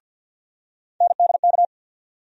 Morse code, heard as a keyed tone, about 47 wpm, 700 Hz. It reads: NDK
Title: USA Call Sign Postfixes - 3 Letters 50wpm